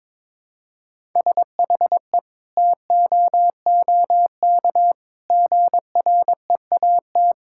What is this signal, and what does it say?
Morse code, heard as a keyed tone, about 22 wpm, 700 Hz. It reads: SHE TOOK GREAT
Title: Sentences from Top 1000 Words - Review Part 6 - 22wpm